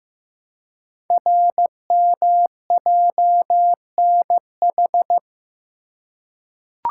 Morse code, heard as a keyed tone, about 15 wpm, 700 Hz. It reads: RMJNH
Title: Code Groups - 5 Letters - 15wpm